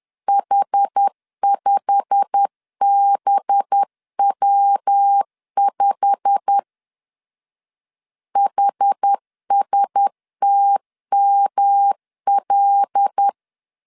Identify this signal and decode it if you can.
Morse code, heard as a keyed tone, about 10 wpm, 800 Hz. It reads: H5BW5 HSTML
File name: LEZIONI / 8 YJ45 / 8 YJ45 10wpm 2549